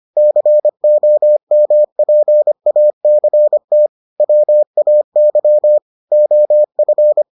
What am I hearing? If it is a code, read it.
Morse code, heard as a keyed tone, about 25 wpm, 600 Hz. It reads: COMPACT WAY OF